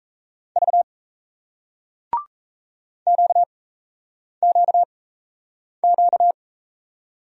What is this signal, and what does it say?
Morse code, heard as a keyed tone, about 40 wpm, spaced as for 8 wpm, 700 Hz. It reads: V QQQ